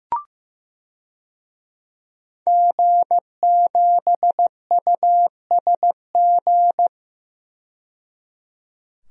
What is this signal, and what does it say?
Morse code, heard as a keyed tone, about 15 wpm, 700 Hz. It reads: G7USG
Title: DXCC-100 Call Signs - 5 Characters - 15wpm